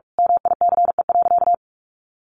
Morse code, heard as a keyed tone, about 45 wpm, 700 Hz. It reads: MICE.
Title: A Sound Of Thunder 45wpm